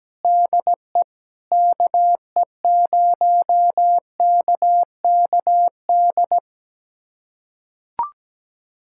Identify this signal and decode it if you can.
Morse code, heard as a keyed tone, about 17 wpm, 700 Hz. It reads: DE KE0KKD
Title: USA Callsign DE Callsign - 6 Characters - 17wpm